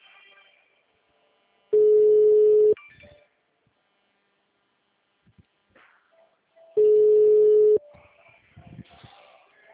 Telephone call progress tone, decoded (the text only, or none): ringback tone